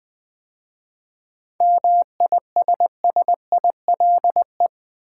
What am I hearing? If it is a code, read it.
Morse code, heard as a keyed tone, about 20 wpm, 700 Hz. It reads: MISSILE
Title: 7 Letter Words - First 3 Letters Given - 20wpm